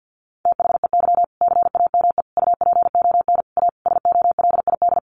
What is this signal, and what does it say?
Morse code, heard as a keyed tone, about 50 wpm, 700 Hz. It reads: THEY CAME UPON A SOLID
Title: Sentences from Top 1500 Words 50wpm